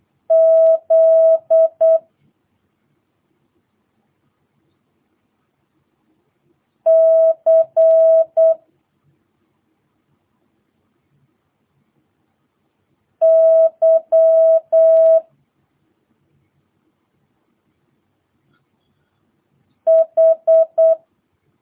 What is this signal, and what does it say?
Morse code, heard as a keyed tone, about 8 wpm, 650 Hz. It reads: Z C Y H
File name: jh8cbh / 2022cw